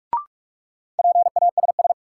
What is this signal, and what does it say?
Morse code, heard as a keyed tone, about 46 wpm, 700 Hz. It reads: PASS
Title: Sentences from Top 400 Words - New Word Drill - 45wpm